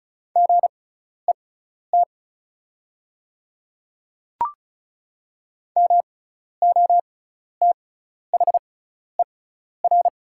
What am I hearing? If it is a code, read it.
Morse code, heard as a keyed tone, about 36 wpm, spaced as for 12 wpm, 690 Hz. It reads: GET MOTHER
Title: 6x Character Spacing - Top 200 Words - 35wpm